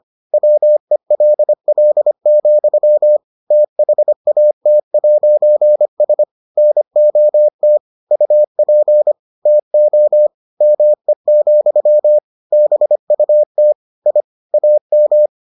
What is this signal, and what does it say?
Morse code, heard as a keyed tone, about 25 wpm, 600 Hz. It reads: WELL, THAT'S NOT UP TO ME, BUT I AM